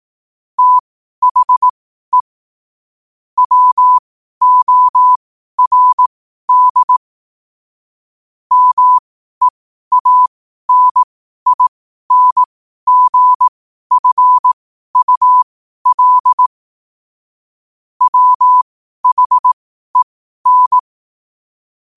Morse code, heard as a keyed tone, about 18 wpm, spaced as for 12 wpm, 1000 Hz. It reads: THE WORD MEANINGFUL WHEN
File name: qotdfiles / 12wpm